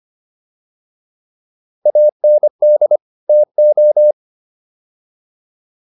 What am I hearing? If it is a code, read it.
Morse code, heard as a keyed tone, about 25 wpm, 600 Hz. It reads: AND TO
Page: intermediate-practice-files